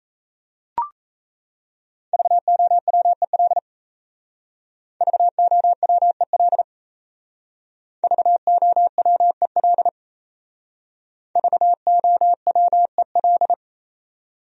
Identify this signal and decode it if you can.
Morse code, heard as a keyed tone, about 35 wpm, 700 Hz. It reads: VOWEL VOWEL VOWEL VOWEL
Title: Speed-Racing Top 500 Words 28wpm